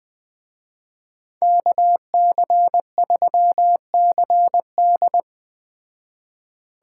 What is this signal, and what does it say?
Morse code, heard as a keyed tone, about 20 wpm, 700 Hz. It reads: KC3CD